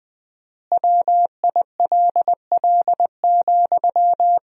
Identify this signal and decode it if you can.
Morse code, heard as a keyed tone, about 20 wpm, 700 Hz. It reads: WILL,
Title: 4x Word-Spacing - Sentences from Top 200 Words - 20wpm